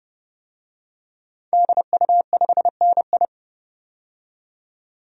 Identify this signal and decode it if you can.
Morse code, heard as a keyed tone, about 30 wpm, 700 Hz. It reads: DU5NI
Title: DXCC-100 Call Signs - 5 Characters - 30wpm